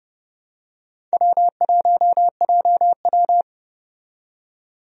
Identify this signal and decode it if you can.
Morse code, heard as a keyed tone, about 30 wpm, 700 Hz. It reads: W1JW
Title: USA Call Signs - 4 Characters - 30wpm